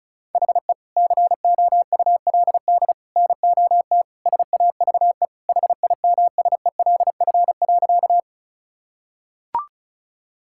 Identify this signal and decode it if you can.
Morse code, heard as a keyed tone, about 34 wpm, 700 Hz. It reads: HE COULD NOT SAVE HIMSELF.
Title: Sentences from Top 1600 Words 35wpm